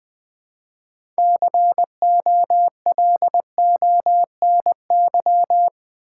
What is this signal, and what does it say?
Morse code, heard as a keyed tone, about 20 wpm, 700 Hz. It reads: COLONY
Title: 6 Letter Words - First 3 Letters Given - 20wpm